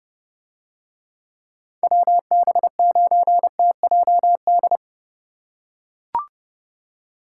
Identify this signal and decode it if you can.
Morse code, heard as a keyed tone, about 29 wpm, 700 Hz. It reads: WB9TJD